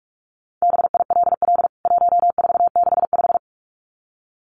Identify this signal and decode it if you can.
Morse code, heard as a keyed tone, about 44 wpm, 700 Hz. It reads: BILL 1465
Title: CWT Exchange 45wpm